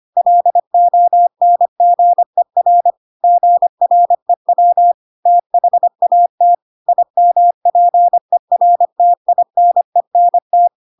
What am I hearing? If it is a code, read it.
Morse code, heard as a keyed tone, about 25 wpm, 700 Hz. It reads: LONGER GREW THAT IMPERTINENT
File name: Pinocchio / Pinocchio ch3 25 25 ik1hge